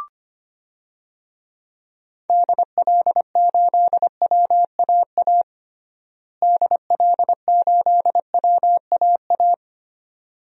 Morse code, heard as a keyed tone, about 25 wpm, 700 Hz. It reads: DL8WAA DL8WAA
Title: DXCC-100 Call Signs - 6 Characters - Repeated 2x - 25wpm